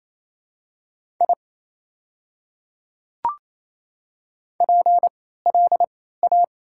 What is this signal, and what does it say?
Morse code, heard as a keyed tone, about 28 wpm, spaced as for 16 wpm, 700 Hz. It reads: I PLA
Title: 3x Character Spacing - Top 200 Words - 28wpm